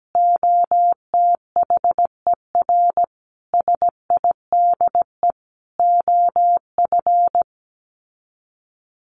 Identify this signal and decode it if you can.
Morse code, heard as a keyed tone, about 17 wpm, 700 Hz. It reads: OTHER SIDE OF
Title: Sets of 5 Words Encore 17wpm